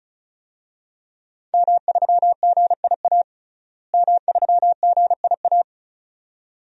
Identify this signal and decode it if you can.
Morse code, heard as a keyed tone, about 34 wpm, 700 Hz. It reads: M3GIA M3GIA
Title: DXCC-100 Call Signs - 5 Characters - Repeated 2x - 35wpm